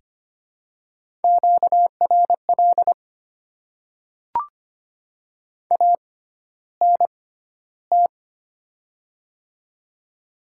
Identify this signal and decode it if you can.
Morse code, heard as a keyed tone, about 25 wpm, spaced as for 13 wpm, 700 Hz. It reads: QRL ANT
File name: Top 20 QSO Elements - 6x Character Spacing - 25wpm